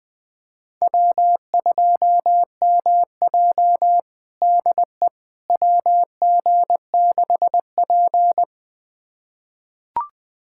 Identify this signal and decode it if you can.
Morse code, heard as a keyed tone, about 21 wpm, 700 Hz. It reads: W2MJ DE WG6P
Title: USA Callsign DE Callsign - 4 Characters - 20wpm